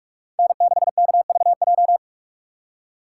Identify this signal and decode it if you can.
Morse code, heard as a keyed tone, about 46 wpm, 700 Hz. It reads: N6KVJ
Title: USA Call Signs - 5 Characters - 6x Character Spacing - 45wpm